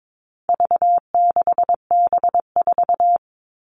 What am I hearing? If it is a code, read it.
Morse code, heard as a keyed tone, about 22 wpm, 700 Hz. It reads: V6B4
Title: All Cut Numbers - 4-Digit 22wpm